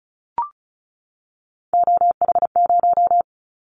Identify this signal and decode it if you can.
Morse code, heard as a keyed tone, about 36 wpm, 700 Hz. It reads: OH0
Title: Callsign Prefixes for Top 100 DXCC - 35wpm